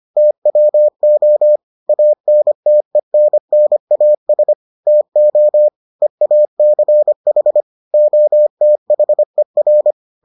Morse code, heard as a keyed tone, about 25 wpm, 600 Hz. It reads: TWO ANTENNAS TO EACH OTHER